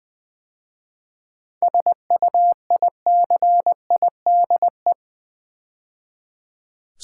Morse code, heard as a keyed tone, about 20 wpm, 700 Hz. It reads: SUICIDE
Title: 7 Letter Words - First 3 Letters Given - 20wpm